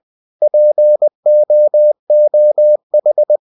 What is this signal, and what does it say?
Morse code, heard as a keyed tone, about 20 wpm, 600 Hz. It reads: POOH
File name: cw-books / winnie-the-pooh / winnie-20wpm-page-0098